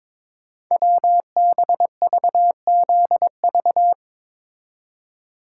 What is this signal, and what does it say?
Morse code, heard as a keyed tone, about 22 wpm, 700 Hz. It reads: WBVZV
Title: Code Groups - 5 Letters - 22wpm